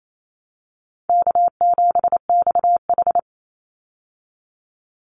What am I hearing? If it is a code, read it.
Morse code, heard as a keyed tone, about 28 wpm, 700 Hz. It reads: K7XH
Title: Call Signs Repeated 3x - 28wpm